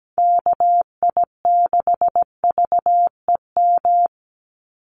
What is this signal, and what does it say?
Morse code, heard as a keyed tone, about 17 wpm, 700 Hz. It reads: KI6VEM
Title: USA Call Signs 17wpm